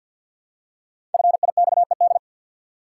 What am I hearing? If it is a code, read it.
Morse code, heard as a keyed tone, about 53 wpm, 700 Hz. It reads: FIXED
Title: Five Letter Words - 50wpm